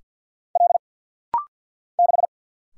Morse code, heard as a keyed tone, about 50 wpm, 700 Hz. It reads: L 6
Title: Single Letter-Number - Rapid-Fire 50wpm